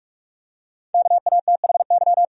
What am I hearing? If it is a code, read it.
Morse code, heard as a keyed tone, about 45 wpm, 700 Hz. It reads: KATHY
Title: USA Common Names - 45wpm